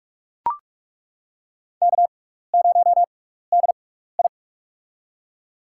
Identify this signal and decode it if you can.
Morse code, heard as a keyed tone, about 42 wpm, spaced as for 15 wpm, 700 Hz. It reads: K0DI
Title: USA Call Signs - 4 Characters - 6x Character Spacing - 45wpm